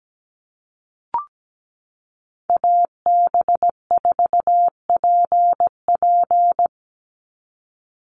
Morse code, heard as a keyed tone, about 17 wpm, 700 Hz. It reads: AB4PP
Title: NA SOTA Chaser Callsigns 17wpm